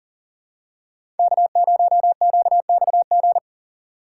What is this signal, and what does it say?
Morse code, heard as a keyed tone, about 40 wpm, 700 Hz. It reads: K0QXG
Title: USA Call Signs - 5 Characters - 2x Character Spacing - 40wpm